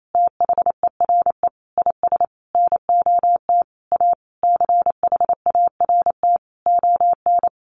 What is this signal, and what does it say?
Morse code, heard as a keyed tone, about 28 wpm, 700 Hz. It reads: THERE IS NOT A CHART ON
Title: Sentences from Top 1300 Words 28wpm